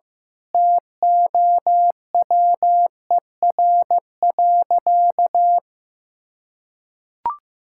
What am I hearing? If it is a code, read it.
Morse code, heard as a keyed tone, about 15 wpm, 700 Hz. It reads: TOWER.
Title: Pangram Challenge v2 - 15wpm